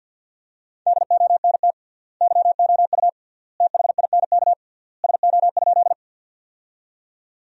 Morse code, heard as a keyed tone, about 48 wpm, 700 Hz. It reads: DONT YOU THINK SO?
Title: 3x Word-Spacing - Sentences from Top 300 Words - 50wpm